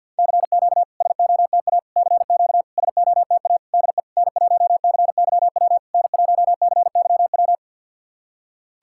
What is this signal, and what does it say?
Morse code, heard as a keyed tone, about 50 wpm, 700 Hz. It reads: CQ SOTA CQ SOTA DE N1CYW N1CYW
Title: SOTA QSO Practice 50wpm